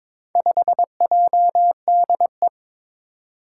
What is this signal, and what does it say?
Morse code, heard as a keyed tone, about 22 wpm, 700 Hz. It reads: 5JDE
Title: Four Challenging Letter-Number Combinations 22wpm